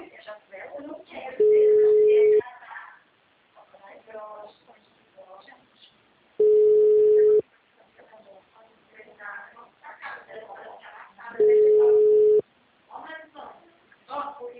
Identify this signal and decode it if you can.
Ringback tone (Europe, 425 Hz), with speech.
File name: out- -2015- - - .50568